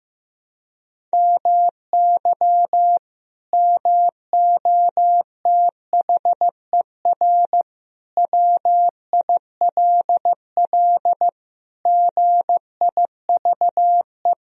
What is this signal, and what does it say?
Morse code, heard as a keyed tone, about 15 wpm, 700 Hz. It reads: MY MOTHER WILL GIVE
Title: Sentences from Top 1000 Words - Review Part 1 - 15wpm